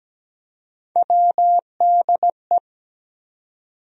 Morse code, heard as a keyed tone, about 17 wpm, 700 Hz. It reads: WDE
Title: USA Call Sign Postfixes - 3 Letters 17wpm